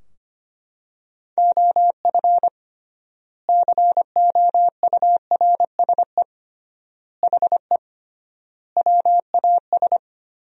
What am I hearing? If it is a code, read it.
Morse code, heard as a keyed tone, about 25 wpm, 700 Hz. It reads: OF COURSE HE WAS
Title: 3x Word-Spacing - Sentences from Top 500 Words - 25wpm